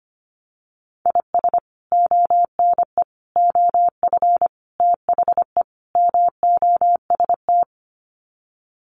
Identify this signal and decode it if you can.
Morse code, heard as a keyed tone, about 25 wpm, 700 Hz. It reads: IS ONE OF THE MOST
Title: Sets of 5 Words 25wpm